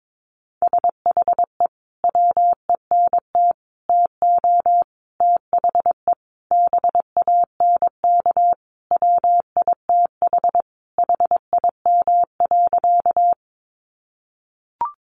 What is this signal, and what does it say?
Morse code, heard as a keyed tone, about 22 wpm, 700 Hz. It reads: SHE WENT TO THE BANK WITH HIM.
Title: Sentences from Top 900 Words 22wpm